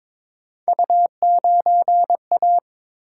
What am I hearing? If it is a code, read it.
Morse code, heard as a keyed tone, about 22 wpm, 700 Hz. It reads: U9A
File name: practice / 22wpm / Three Letter-Number Combinations - 22wpm